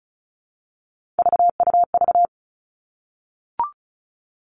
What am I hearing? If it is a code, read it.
Morse code, heard as a keyed tone, about 36 wpm, 700 Hz. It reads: VUV
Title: Three Challenging Letter-Number Combinations 35wpm